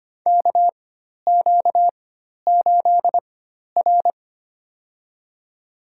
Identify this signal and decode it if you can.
Morse code, heard as a keyed tone, about 25 wpm, spaced as for 12 wpm, 700 Hz. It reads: KQ8R
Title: USA Call Signs - 4x Character Spacing - 25wpm